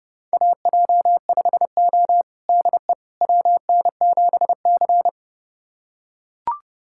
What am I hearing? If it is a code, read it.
Morse code, heard as a keyed tone, about 30 wpm, 700 Hz. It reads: AJ5O DE WN7C
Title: USA Callsign DE Callsign - 4 Characters - 30wpm